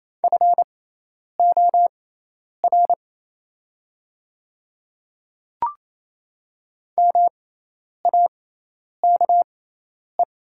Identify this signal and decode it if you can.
Morse code, heard as a keyed tone, about 28 wpm, spaced as for 10 wpm, 700 Hz. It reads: FOR MAKE
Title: 6x Character Spacing - Top 100 Words - 28wpm